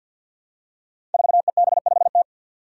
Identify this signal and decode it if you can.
Morse code, heard as a keyed tone, about 52 wpm, 700 Hz. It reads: 4EB5T
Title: Five Challenging Letter-Number Combinations 50wpm